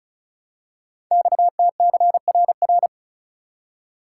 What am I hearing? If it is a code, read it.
Morse code, heard as a keyed tone, about 36 wpm, 700 Hz. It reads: XTCRR
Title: Code Groups - 5 Letters - 35wpm